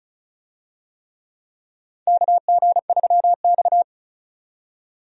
Morse code, heard as a keyed tone, about 36 wpm, 700 Hz. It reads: KG3X
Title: USA Call Signs - 4 Characters - 35wpm